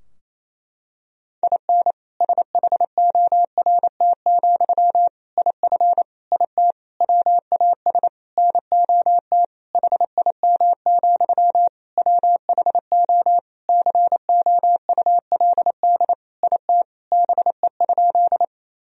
Morse code, heard as a keyed tone, about 28 wpm, 700 Hz. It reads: IN SHORT, IF IT WAS NOT HIM, WHO COULD IT BE?
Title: Sentences from Top 1000 Words - Review Part 2 - 28wpm